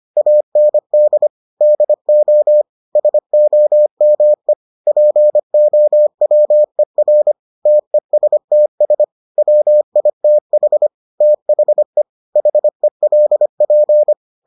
Morse code, heard as a keyed tone, about 25 wpm, 600 Hz. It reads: AND DO SOME POWER TESTS WITH THE HELP